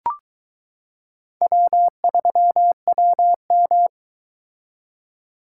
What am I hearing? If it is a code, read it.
Morse code, heard as a keyed tone, about 21 wpm, 700 Hz. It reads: W3WM